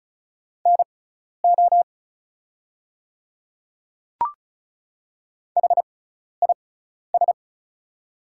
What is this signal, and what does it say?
Morse code, heard as a keyed tone, about 36 wpm, spaced as for 12 wpm, 690 Hz. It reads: NO HIS
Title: 6x Character Spacing - Top 100 Words - 35wpm